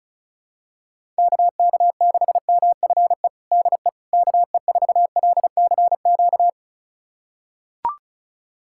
Morse code, heard as a keyed tone, about 36 wpm, 700 Hz. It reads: KK6MFE DE KE4LCQ
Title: USA Callsign DE Callsign - 6 Characters - 35wpm